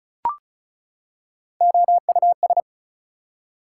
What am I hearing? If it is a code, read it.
Morse code, heard as a keyed tone, about 36 wpm, 700 Hz. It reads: OUS
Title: Prefixes and Suffixes 35wpm